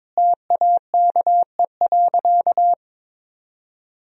Morse code, heard as a keyed tone, about 22 wpm, 700 Hz. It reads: TAKE.
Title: Sentences from Top 1000 Words - Review Part 4 - 22wpm